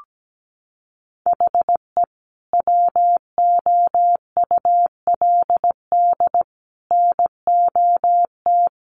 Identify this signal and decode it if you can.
Morse code, heard as a keyed tone, about 17 wpm, 700 Hz. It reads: HE WOULD NOT
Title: Sentences from Top 1500 Words 17wpm